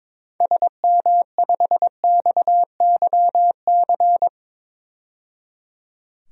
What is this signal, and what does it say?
Morse code, heard as a keyed tone, about 21 wpm, 700 Hz. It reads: SM5XYC
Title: DXCC-100 Call Signs - 6 Characters - 22wpm